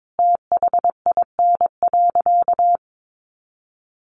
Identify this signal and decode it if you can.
Morse code, heard as a keyed tone, about 22 wpm, 700 Hz. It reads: THIN.
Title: Sentences from Top 1200 Words 22wpm